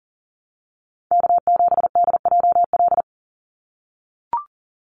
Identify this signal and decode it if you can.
Morse code, heard as a keyed tone, about 40 wpm, 700 Hz. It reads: K7DJL
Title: NA SOTA Chaser Callsigns 40wpm